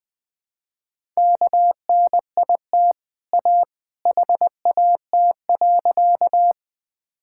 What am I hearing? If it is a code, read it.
Morse code, heard as a keyed tone, about 20 wpm, 700 Hz. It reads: KNIT A HAT.
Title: Easy Sentences with Hints - 20wpm